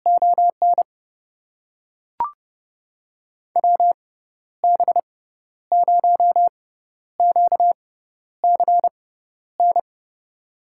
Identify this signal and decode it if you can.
Morse code, heard as a keyed tone, about 30 wpm, spaced as for 12 wpm, 700 Hz. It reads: ON WB0QCN
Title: USA Call Signs - 6x Character Spacing - 30wpm